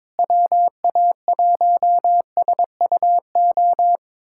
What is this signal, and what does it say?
Morse code, heard as a keyed tone, about 22 wpm, 700 Hz. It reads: WA1SUO
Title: USA Callsign DE Callsign - 6 Characters - 22wpm